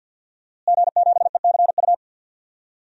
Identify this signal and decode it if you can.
Morse code, heard as a keyed tone, about 51 wpm, 700 Hz. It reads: G7ECU
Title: DXCC-100 Call Signs - 5 Characters - Repeated 3x - 50wpm